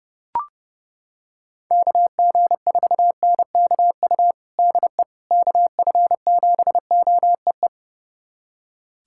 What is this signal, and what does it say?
Morse code, heard as a keyed tone, about 30 wpm, 700 Hz. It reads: KG4NKU DE KF7OEE
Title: USA Callsign DE Callsign - 6 Characters - 30wpm